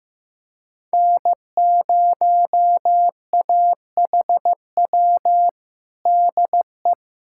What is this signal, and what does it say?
Morse code, heard as a keyed tone, about 15 wpm, 700 Hz. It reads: N0AHW DE
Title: USA Callsign DE Callsign - 5 Characters - 15wpm